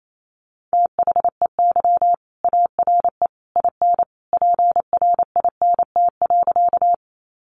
Morse code, heard as a keyed tone, about 28 wpm, 700 Hz. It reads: THEY ARE IN PRINT.
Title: Sentences from Top 1400 Words 28wpm